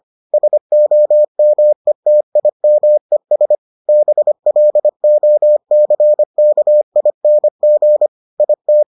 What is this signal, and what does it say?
Morse code, heard as a keyed tone, about 25 wpm, 600 Hz. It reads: SOMETIMES BLOCKING IT